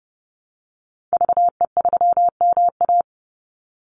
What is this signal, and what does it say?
Morse code, heard as a keyed tone, about 30 wpm, 700 Hz. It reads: VE3MA
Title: Call Signs - 30wpm